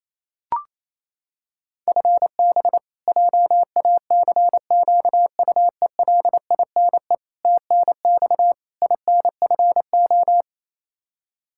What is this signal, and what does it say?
Morse code, heard as a keyed tone, about 28 wpm, 700 Hz. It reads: FB JACQUELINE TNX INFO
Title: Practice QSO - Exchange 3 - More Info - 28wpm